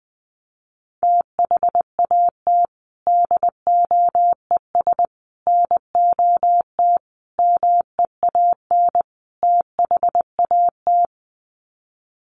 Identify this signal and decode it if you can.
Morse code, heard as a keyed tone, about 20 wpm, 700 Hz. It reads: THAT DOES NOT MEAN THAT
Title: Sets of 5 Words Encore 20wpm